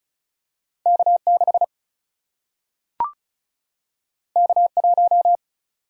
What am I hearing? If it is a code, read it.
Morse code, heard as a keyed tone, about 36 wpm, 700 Hz. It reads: K 6 K 1
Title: Single-Letter Single-Number 35wpm